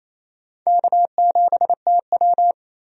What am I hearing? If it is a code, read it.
Morse code, heard as a keyed tone, about 28 wpm, 700 Hz. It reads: K7TW
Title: USA Call Signs - 4 Characters - 28wpm